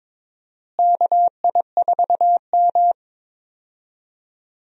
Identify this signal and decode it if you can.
Morse code, heard as a keyed tone, about 21 wpm, 700 Hz. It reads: KI4M
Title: USA Call Signs - 4 Characters - Repeated 2x - 22wpm